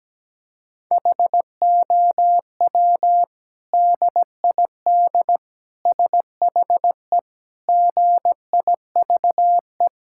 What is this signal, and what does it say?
Morse code, heard as a keyed tone, about 17 wpm, 700 Hz. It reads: HOW DID SHE GIVE
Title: Sentences from Top 1000 Words - Review Part 5 - 17wpm